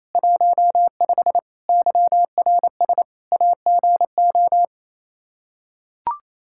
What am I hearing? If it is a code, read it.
Morse code, heard as a keyed tone, about 28 wpm, 700 Hz. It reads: 15 YRS AGO
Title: Practice QSO - Exchange 3 - More Info - 28wpm